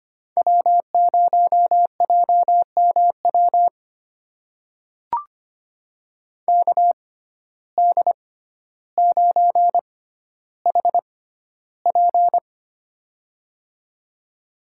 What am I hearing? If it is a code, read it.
Morse code, heard as a keyed tone, about 25 wpm, spaced as for 13 wpm, 700 Hz. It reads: W0JMW KD9HP
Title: USA Call Signs - 6x Character Spacing - 25wpm